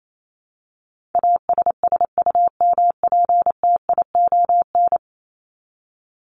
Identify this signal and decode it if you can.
Morse code, heard as a keyed tone, about 28 wpm, 700 Hz. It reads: ASSUMPTION
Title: Top 1751-2000 Words 28wpm